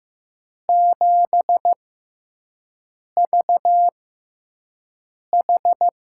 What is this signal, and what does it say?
Morse code, heard as a keyed tone, about 15 wpm, spaced as for 5 wpm, 700 Hz. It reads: 7VH
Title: USA Call Signs - 4 Characters - 6x Character Spacing - 15wpm